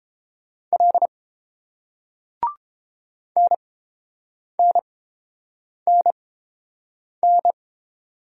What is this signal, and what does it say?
Morse code, heard as a keyed tone, about 30 wpm, spaced as for 7 wpm, 700 Hz. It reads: L NNNN